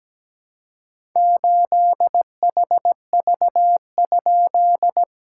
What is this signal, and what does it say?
Morse code, heard as a keyed tone, about 17 wpm, 700 Hz. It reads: 8HV?
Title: SOTA QSO Practice 17wpm